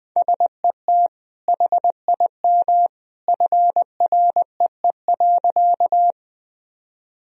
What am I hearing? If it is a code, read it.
Morse code, heard as a keyed tone, about 20 wpm, 700 Hz. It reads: SET HIM FREE.
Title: Sentences from Top 1000 Words - Review Part 1 - 20wpm